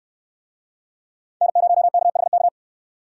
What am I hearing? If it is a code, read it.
Morse code, heard as a keyed tone, about 65 wpm, 700 Hz. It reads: N0KHK